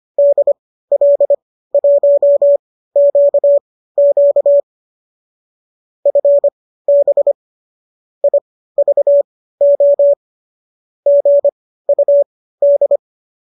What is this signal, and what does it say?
Morse code, heard as a keyed tone, about 25 wpm, spaced as for 15 wpm, 570 Hz. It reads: DL1QQ FB IVO GUD
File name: qso219 15